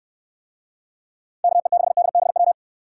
Morse code, heard as a keyed tone, about 64 wpm, 700 Hz. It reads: KE6DBW